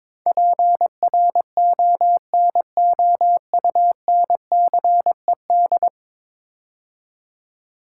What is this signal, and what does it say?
Morse code, heard as a keyed tone, about 22 wpm, 700 Hz. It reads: PRONOUNCED
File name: practice / 22wpm / 10 Letter Words - First 3 Letters Given - 22wpm